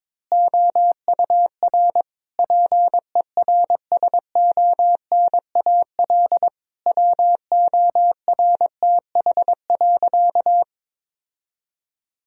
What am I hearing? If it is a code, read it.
Morse code, heard as a keyed tone, about 22 wpm, 700 Hz. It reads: OUR PERSONAL WORTH.
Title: Sentences from Top 1000 Words - Review Part 4 - 22wpm